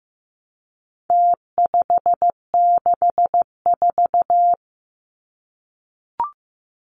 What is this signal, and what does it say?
Morse code, heard as a keyed tone, about 15 wpm, 700 Hz. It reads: T564
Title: Numbers - Contest Style - 4-Digit 15wpm